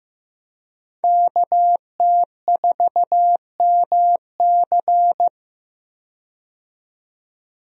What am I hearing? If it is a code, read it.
Morse code, heard as a keyed tone, about 15 wpm, 700 Hz. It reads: KT4MC
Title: USA Call Signs - 5 Characters - 15wpm